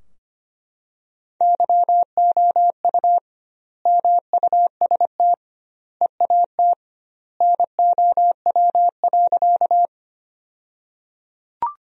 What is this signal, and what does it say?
Morse code, heard as a keyed tone, about 25 wpm, 700 Hz. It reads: YOU MUST EAT NOW.
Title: 2x Word-Spacing - Sentences from Top 300 Words - 25wpm